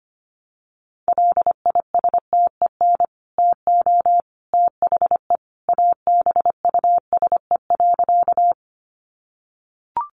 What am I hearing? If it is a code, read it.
Morse code, heard as a keyed tone, about 25 wpm, 700 Hz. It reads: LISTEN TO THE ABUSE.
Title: Sentences from Top 1300 Words 25wpm